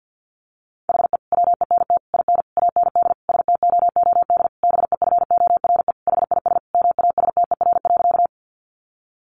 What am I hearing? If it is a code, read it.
Morse code, heard as a keyed tone, about 50 wpm, 700 Hz. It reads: HE WENT IN AND STOOD BEFORE HIS MASTER.
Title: Sentences from Top 1300 Words 50wpm